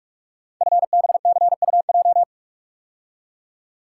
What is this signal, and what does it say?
Morse code, heard as a keyed tone, about 44 wpm, 700 Hz. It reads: FBCUJ
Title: Code Groups - 5 Letters - 45wpm